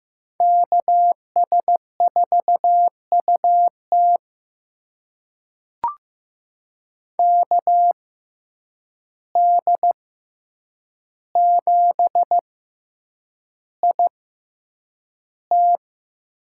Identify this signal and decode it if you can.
Morse code, heard as a keyed tone, about 15 wpm, spaced as for 8 wpm, 700 Hz. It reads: KS4UT KD7IT
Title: USA Call Signs - 5 Characters - 6x Character Spacing - 15wpm